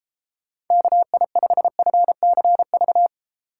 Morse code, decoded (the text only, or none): KI5FCV